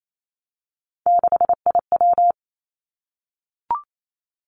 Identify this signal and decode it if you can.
Morse code, heard as a keyed tone, about 28 wpm, 700 Hz. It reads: 6IW
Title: Three Challenging Letter-Number Combinations 28wpm